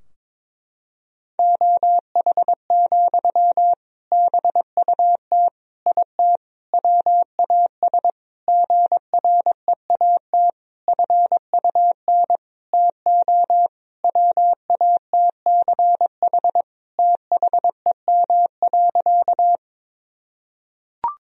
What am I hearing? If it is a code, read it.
Morse code, heard as a keyed tone, about 22 wpm, 700 Hz. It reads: OH, BUT IT WAS GREAT FUN TO WATCH THEM.
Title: Sentences from Top 1000 Words - Review Part 4 - 22wpm